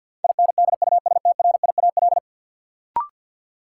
Morse code, heard as a keyed tone, about 52 wpm, 700 Hz. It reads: INDUSTRIAL
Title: Sets of 5 Words - New Word Drill - 50wpm